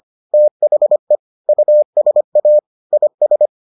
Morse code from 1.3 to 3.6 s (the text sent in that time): USA IS